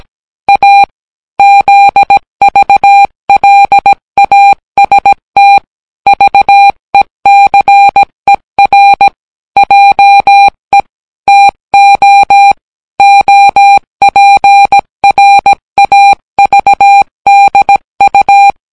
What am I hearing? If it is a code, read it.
Morse code, heard as a keyed tone, about 17 wpm, 800 Hz. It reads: A ZVLAST VECER JE TO OPRAVDU